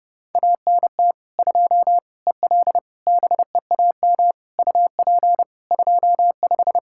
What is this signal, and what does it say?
Morse code, heard as a keyed tone, about 32 wpm, 700 Hz. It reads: ANT 2 EL BEAM UP 25
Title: Practice QSO - Exchange 2 - Info - 30wpm